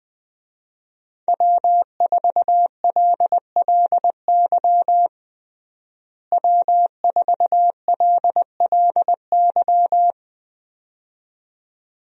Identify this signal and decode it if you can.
Morse code, heard as a keyed tone, about 20 wpm, 700 Hz. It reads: W4LLY W4LLY
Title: USA Call Signs - 5 Characters - Repeated 2x - 20wpm